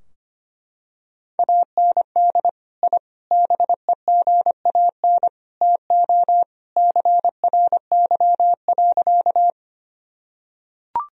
Morse code, heard as a keyed tone, about 25 wpm, 700 Hz. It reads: AND I BEGAN TO CRY.
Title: Sentences from Top 1000 Words - Review Part 2 - 25wpm